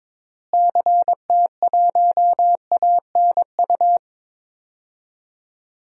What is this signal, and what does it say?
Morse code, heard as a keyed tone, about 21 wpm, 700 Hz. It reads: CT1ANU
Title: DXCC-100 Call Signs - 6 Characters - Repeated 3x - 22wpm